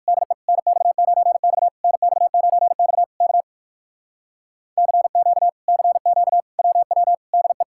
Morse code, heard as a keyed tone, about 56 wpm, 700 Hz. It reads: DE NX9X NX9X K CQ CQ WW DE
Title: Speed-Racing Common QSO Phrases - 45wpm